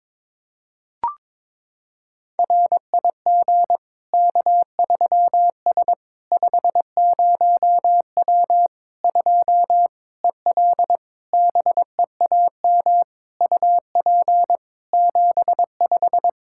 Morse code, heard as a keyed tone, about 22 wpm, 700 Hz. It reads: RIG K3S 50W 2 EL BEAM UP 75
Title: Practice QSO - Exchange 2 - Info - 22wpm